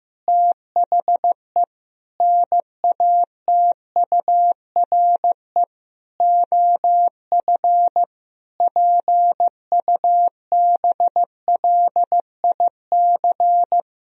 Morse code, heard as a keyed tone, about 15 wpm, 700 Hz. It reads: THE NATURE OF PUBLIC
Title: Sentences from Top 1000 Words - Review Part 3 - 15wpm